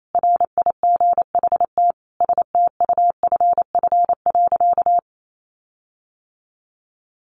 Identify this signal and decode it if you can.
Morse code, heard as a keyed tone, about 28 wpm, 700 Hz. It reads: RIGHT STUFF.
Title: Sentences from Top 1400 Words 28wpm